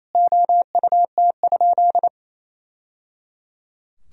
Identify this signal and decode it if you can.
Morse code, heard as a keyed tone, about 28 wpm, 700 Hz. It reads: OUT?
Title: Sentences from Top 1000 Words - Review Part 3 - 28wpm